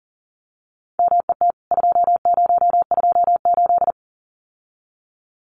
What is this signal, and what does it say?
Morse code, heard as a keyed tone, about 40 wpm, 700 Hz. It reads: MET 2028
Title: CWT Exchange 40wpm